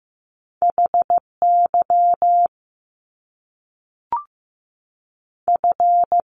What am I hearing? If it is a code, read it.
Morse code, heard as a keyed tone, about 15 wpm, 700 Hz. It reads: H Y F
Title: Top Two Letter Combinations in Words 15wpm